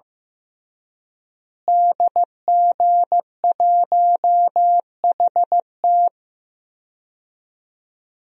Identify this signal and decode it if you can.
Morse code, heard as a keyed tone, about 15 wpm, 700 Hz. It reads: DG1HT
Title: DXCC-100 Call Signs - 5 Characters - Repeated 3x - 15wpm